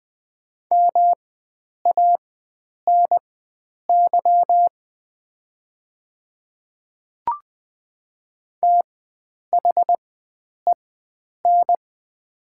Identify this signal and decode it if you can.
Morse code, heard as a keyed tone, about 21 wpm, spaced as for 9 wpm, 700 Hz. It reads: MANY THEN
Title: 4x Character Spacing - Top 300 Words - 20wpm